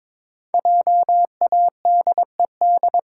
Morse code, heard as a keyed tone, about 22 wpm, 700 Hz. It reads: JADED
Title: Pangram Challenge v2 - 22wpm